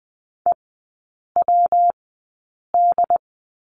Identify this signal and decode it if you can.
Morse code, heard as a keyed tone, about 20 wpm, spaced as for 8 wpm, 700 Hz. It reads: EWD
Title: 114 Lesson 19e - Letter w - Three Characters 20wpm